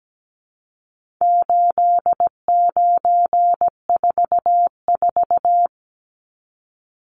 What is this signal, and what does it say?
Morse code, heard as a keyed tone, about 17 wpm, 700 Hz. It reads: 8944
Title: Numbers - 4-Digit 17wpm